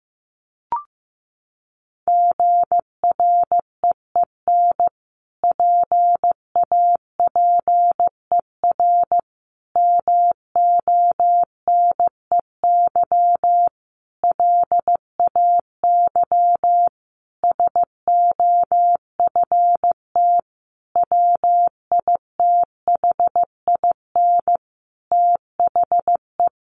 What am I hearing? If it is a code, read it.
Morse code, heard as a keyed tone, about 15 wpm, 700 Hz. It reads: GREEN PAPER MONEY LAY SOFT WITHIN THE
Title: Sentences from Top 1500 Words 15wpm